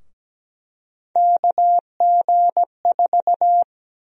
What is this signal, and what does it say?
Morse code, heard as a keyed tone, about 17 wpm, 700 Hz. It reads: KG4
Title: USA Call Sign Prefixes - Repeated 2x - 17wpm